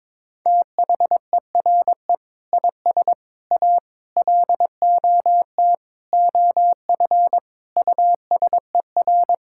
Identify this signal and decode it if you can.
Morse code, heard as a keyed tone, about 22 wpm, 700 Hz. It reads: THERE IS A LOT OF USER
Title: Sentences from Top 1000 Words - Review Part 6 - 22wpm